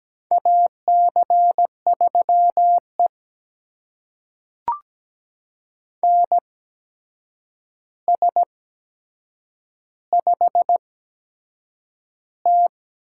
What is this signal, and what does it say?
Morse code, heard as a keyed tone, about 17 wpm, spaced as for 7 wpm, 700 Hz. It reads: AC3E NS5T
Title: USA Call Signs - 4 Characters - 8x Character Spacing - 17wpm